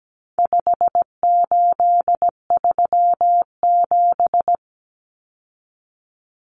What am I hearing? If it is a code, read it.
Morse code, heard as a keyed tone, about 17 wpm, 700 Hz. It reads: 5837
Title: Numbers - 4-Digit 17wpm